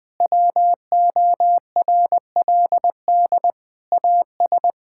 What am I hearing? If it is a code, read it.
Morse code, heard as a keyed tone, about 21 wpm, 700 Hz. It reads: WORLD AS
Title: Sentences with 9-Word Phrases - Spoken Beginning - 20wpm